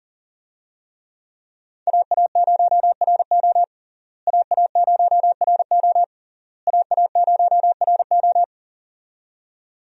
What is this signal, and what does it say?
Morse code, heard as a keyed tone, about 37 wpm, 700 Hz. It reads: AA0RO AA0RO AA0RO
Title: USA Call Signs - 5 Characters - Repeated 3x - 40wpm